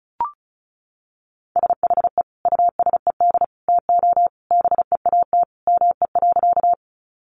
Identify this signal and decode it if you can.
Morse code, heard as a keyed tone, about 34 wpm, 700 Hz. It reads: SHE USED TO BEAT ME.
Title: Sentences from Top 1300 Words 35wpm